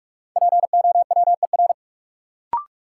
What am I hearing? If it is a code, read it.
Morse code, heard as a keyed tone, about 46 wpm, 700 Hz. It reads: POWER
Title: Sets of 4 Words - New Word Drill - 45wpm